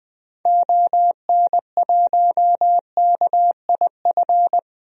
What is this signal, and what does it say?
Morse code, heard as a keyed tone, about 20 wpm, 700 Hz. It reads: ON1KIF
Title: DXCC-100 Call Signs - 6 Characters - 20wpm